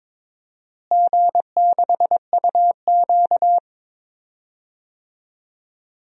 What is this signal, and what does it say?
Morse code, heard as a keyed tone, about 23 wpm, 700 Hz. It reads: G6UQ
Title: DXCC-100 Call Signs - 4 Characters - Repeated 3x - 22wpm